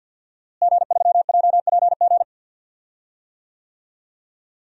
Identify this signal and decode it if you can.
Morse code, heard as a keyed tone, about 52 wpm, 700 Hz. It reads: G3JPG
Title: DXCC-100 Call Signs - 5 Characters - Repeated 2x - 50wpm